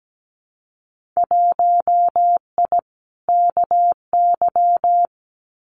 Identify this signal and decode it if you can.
Morse code, heard as a keyed tone, about 17 wpm, 700 Hz. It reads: 1I KY
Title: Winter Field Day Exchanges 17wpm